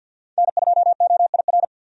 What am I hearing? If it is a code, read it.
Morse code, heard as a keyed tone, about 53 wpm, 700 Hz. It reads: N2OIR
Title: USA Call Signs - 5 Characters - 50wpm